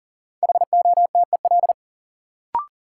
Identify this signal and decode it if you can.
Morse code, heard as a keyed tone, about 38 wpm, 700 Hz. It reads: HOTEL
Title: Sentences from Top 1600 Words - New Word Drill - 40wpm